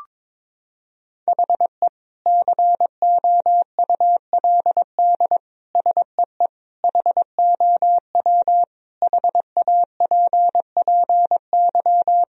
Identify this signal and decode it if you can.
Morse code, heard as a keyed tone, about 22 wpm, 700 Hz. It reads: HE COULD SEE HOW HAPPY
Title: Sentences from Top 1000 Words - Review Part 3 - 22wpm